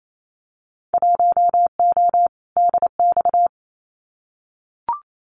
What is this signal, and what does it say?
Morse code, heard as a keyed tone, about 28 wpm, 700 Hz. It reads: 1O DX
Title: Winter Field Day Exchanges Encore 28wpm